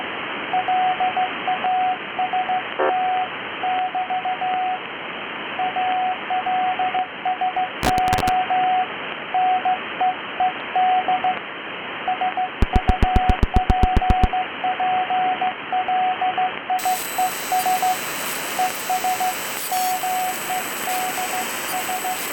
Morse code, heard as a keyed tone, about 15 wpm, 740 Hz, with radio noise. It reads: LAST <BT> ALSO NEED SUPPLIES ES MEDS